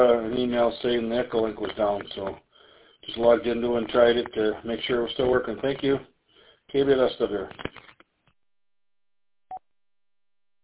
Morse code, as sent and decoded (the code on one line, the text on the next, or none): none